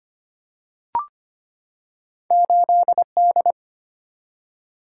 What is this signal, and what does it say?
Morse code, heard as a keyed tone, about 25 wpm, 700 Hz. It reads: 8D
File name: Two Letter-Number Combinations - 25wpm